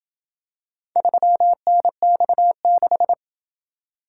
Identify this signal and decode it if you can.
Morse code, heard as a keyed tone, about 27 wpm, 700 Hz. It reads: 3NX6